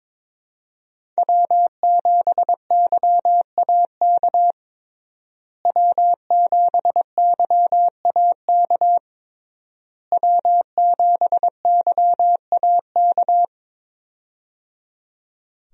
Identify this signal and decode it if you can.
Morse code, heard as a keyed tone, about 21 wpm, 700 Hz. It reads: W7YAK W7YAK W7YAK
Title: USA Call Signs - 5 Characters - Repeated 3x - 22wpm